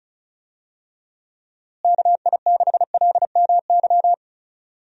Morse code, heard as a keyed tone, about 34 wpm, 700 Hz. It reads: KI6LMY
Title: USA Call Signs - 6 Characters - Repeated 3x - 35wpm